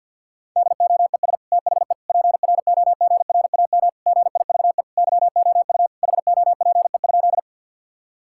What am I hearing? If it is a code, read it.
Morse code, heard as a keyed tone, about 46 wpm, 700 Hz. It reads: DOES THE PROGRAM GIVE YOU HOPE?
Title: Sentences from Top 1000 Words - Review Part 5 - 50wpm